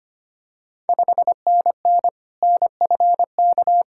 Morse code, heard as a keyed tone, about 25 wpm, 700 Hz. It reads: 5NN NFK
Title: Ontario QSO Party Exchange 25wpm